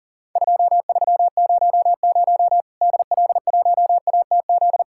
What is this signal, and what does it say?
Morse code, heard as a keyed tone, about 40 wpm, 700 Hz. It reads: 2300 DL1ATZ
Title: Worked All Europe DX Contest - QTC Exchange - 40wpm